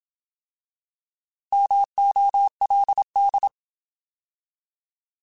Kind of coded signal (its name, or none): Morse code